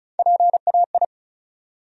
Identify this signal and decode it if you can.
Morse code, heard as a keyed tone, about 36 wpm, 700 Hz. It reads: PAI
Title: USA Call Sign Postfixes - 3 Letters 35wpm